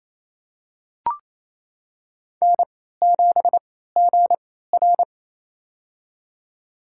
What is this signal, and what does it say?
Morse code, heard as a keyed tone, about 28 wpm, spaced as for 16 wpm, 700 Hz. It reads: N7GR
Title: USA Call Signs - 3x Character Spacing - 28wpm